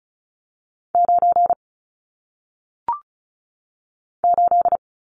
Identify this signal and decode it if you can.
Morse code, heard as a keyed tone, about 36 wpm, 700 Hz. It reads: 9 8
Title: Numbers - 1-Digit 35wpm